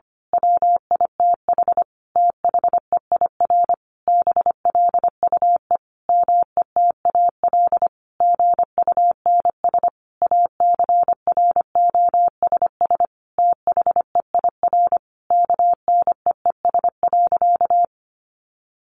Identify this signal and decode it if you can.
Morse code, heard as a keyed tone, about 25 wpm, 700 Hz. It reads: WITH THEIR BLUE METAL GUNS ACROSS THEIR KNEES.
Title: A Sound Of Thunder 25wpm